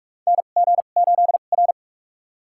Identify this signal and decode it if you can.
Morse code, heard as a keyed tone, about 46 wpm, spaced as for 32 wpm, 700 Hz. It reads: NG8R
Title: USA Call Signs - 4 Characters - 2x Character Spacing - 45wpm